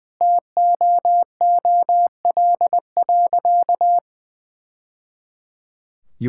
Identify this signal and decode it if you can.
Morse code, heard as a keyed tone, about 20 wpm, 700 Hz. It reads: TOOL.
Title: Sentences from Top 1000 Words - Review Part 6 - 20wpm